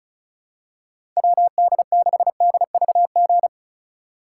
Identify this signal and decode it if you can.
Morse code, heard as a keyed tone, about 36 wpm, 700 Hz. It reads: WD6DVG
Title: USA Call Signs - 6 Characters - 2x Character Spacing - 35wpm